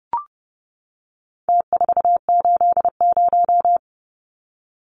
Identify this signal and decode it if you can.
Morse code, heard as a keyed tone, about 30 wpm, 700 Hz. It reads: T480
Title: Numbers - Contest Style - 4-Digit 30wpm